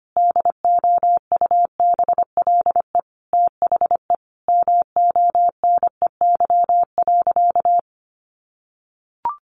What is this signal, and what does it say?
Morse code, heard as a keyed tone, about 25 wpm, 700 Hz. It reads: DOUBLE THE MONEY.
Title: Sentences from Top 1400 Words 25wpm